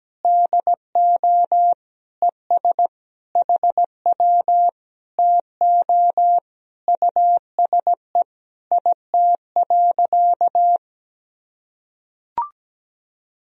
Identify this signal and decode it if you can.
Morse code, heard as a keyed tone, about 17 wpm, 700 Hz. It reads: DO ES HW TO USE IT.
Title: Sentences from Top 500 Words and Abbreviations 17wpm